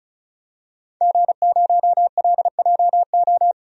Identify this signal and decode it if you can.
Morse code, heard as a keyed tone, about 36 wpm, 700 Hz. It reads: G0LJO
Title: DXCC-100 Call Signs - 5 Characters - 35wpm